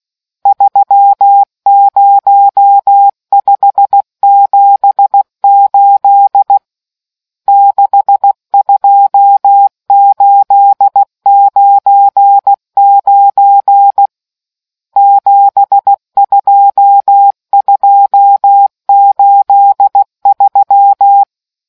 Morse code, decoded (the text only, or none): 30578 62899 72283